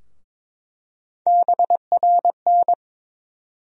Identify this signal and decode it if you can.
Morse code, heard as a keyed tone, about 22 wpm, 700 Hz. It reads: BRN
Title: USA Call Sign Postfixes - 3 Letters 22wpm